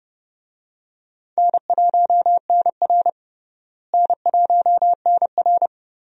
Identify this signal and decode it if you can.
Morse code, heard as a keyed tone, about 30 wpm, 700 Hz. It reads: N1NR N1NR
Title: USA Call Signs - 4 Characters - Repeated 2x - 30wpm